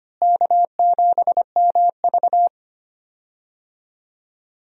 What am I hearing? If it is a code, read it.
Morse code, heard as a keyed tone, about 25 wpm, 700 Hz. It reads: K7MV
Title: USA Call Signs - 4 Characters - Repeated 2x - 25wpm